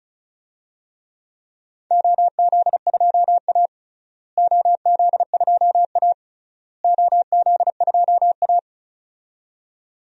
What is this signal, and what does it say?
Morse code, heard as a keyed tone, about 34 wpm, 700 Hz. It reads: OZ2A OZ2A OZ2A
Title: DXCC-100 Call Signs - 4 Characters - Repeated 3x - 35wpm